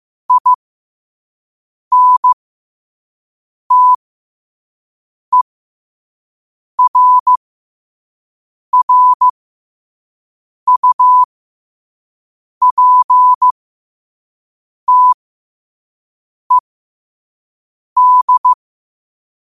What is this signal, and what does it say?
Morse code, heard as a keyed tone, about 15 wpm, spaced as for 5 wpm, 1000 Hz. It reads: INTERRUPTED
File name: qotdfiles / 5wpm